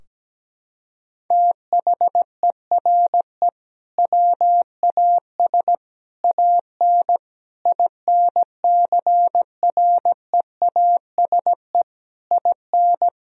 Morse code, heard as a keyed tone, about 17 wpm, 700 Hz. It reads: THERE WAS AN INCREASE IN
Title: Sentences from Top 1000 Words - Review Part 4 - 17wpm